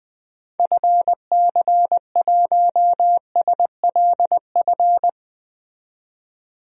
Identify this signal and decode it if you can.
Morse code, heard as a keyed tone, about 20 wpm, 700 Hz. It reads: FC1SLF
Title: DXCC-100 Call Signs - 6 Characters - 20wpm